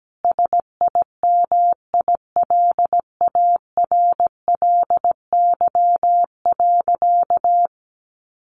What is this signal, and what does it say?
Morse code, heard as a keyed tone, about 17 wpm, 700 Hz. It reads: SIMILARLY.
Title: Sentences from Top 1400 Words 17wpm